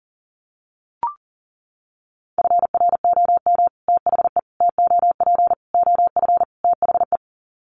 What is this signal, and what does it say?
Morse code, heard as a keyed tone, about 40 wpm, 700 Hz. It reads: FROM THE TOP OF THE
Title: Sets of 5 Words Encore 40wpm